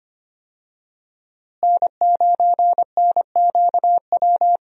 Morse code, heard as a keyed tone, about 25 wpm, 700 Hz. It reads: N9NQW
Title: USA Call Signs - 5 Characters - Repeated 3x - 25wpm